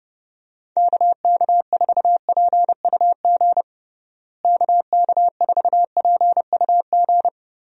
Morse code, heard as a keyed tone, about 32 wpm, 700 Hz. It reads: KK4PUG KK4PUG
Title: USA Call Signs - 6 Characters - Repeated 2x - 30wpm